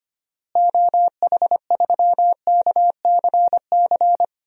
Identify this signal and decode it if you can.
Morse code, heard as a keyed tone, about 25 wpm, 700 Hz. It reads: OH3KCC
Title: DXCC-100 Call Signs - 6 Characters - Repeated 3x - 25wpm